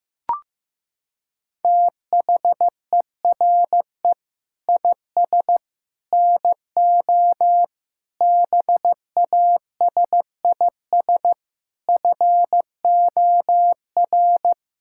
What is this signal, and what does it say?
Morse code, heard as a keyed tone, about 15 wpm, 700 Hz. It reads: THERE IS NO BASIS FOR
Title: Sentences from Top 1600 Words 15wpm